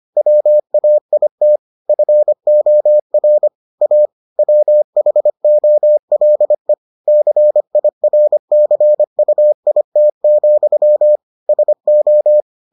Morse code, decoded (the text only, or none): WAIT FOR A WHOLE CIRCUIT, SO